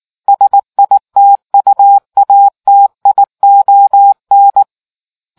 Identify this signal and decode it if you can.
Morse code, heard as a keyed tone, about 19 wpm, 800 Hz. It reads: SITUATION